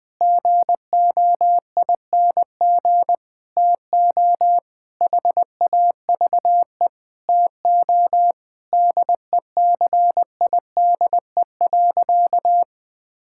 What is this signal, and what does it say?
Morse code, heard as a keyed tone, about 20 wpm, 700 Hz. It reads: GOING TO HAVE TO DECIDE.
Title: Sentences with 4-Word Phrases - Spoken Beginning - 20wpm